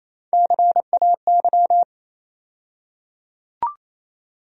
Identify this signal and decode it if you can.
Morse code, heard as a keyed tone, about 28 wpm, 700 Hz. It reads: CAY
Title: USA Call Sign Postfixes - 3 Letters 28wpm